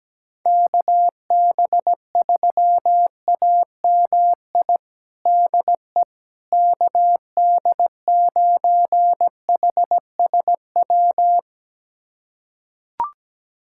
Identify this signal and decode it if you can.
Morse code, heard as a keyed tone, about 17 wpm, 700 Hz. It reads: KB3AMI DE KD9HSW
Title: USA Callsign DE Callsign - 6 Characters - 17wpm